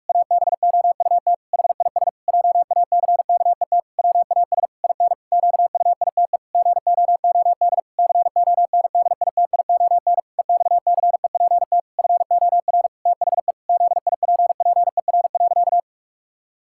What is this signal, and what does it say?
Morse code, heard as a keyed tone, about 46 wpm, 700 Hz. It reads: ABOUT HIS JACKET WAS IN QUITE GOOD CONDITION EXCEPT FOR THE ZIPPER.
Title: Pangram Challenge v2 - 45wpm